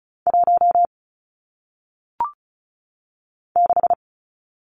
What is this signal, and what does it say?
Morse code, heard as a keyed tone, about 36 wpm, 700 Hz. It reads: 1 6
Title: Numbers - 1-Digit 35wpm